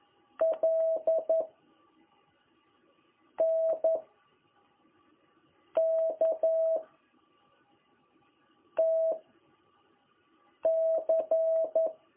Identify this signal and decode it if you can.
Morse code, heard as a keyed tone, about 11 wpm, 660 Hz. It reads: L N K T C
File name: jh8cbh / 2021cw